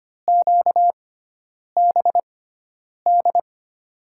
Morse code, heard as a keyed tone, about 25 wpm, spaced as for 9 wpm, 700 Hz. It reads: QBD
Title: USA Call Signs - 6x Character Spacing - 25wpm